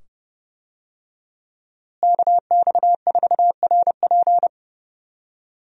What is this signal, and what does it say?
Morse code, heard as a keyed tone, about 30 wpm, 700 Hz. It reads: KX4RP
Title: USA Call Signs - 5 Characters - Repeated 2x - 30wpm